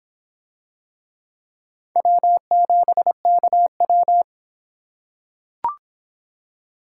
Morse code, heard as a keyed tone, about 23 wpm, 700 Hz. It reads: W7KW